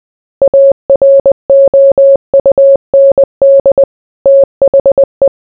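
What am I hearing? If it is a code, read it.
Morse code, heard as a keyed tone, about 20 wpm, 550 Hz. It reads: AROUND THE
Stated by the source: Morse code mixed with noise and music as background sound for sleep.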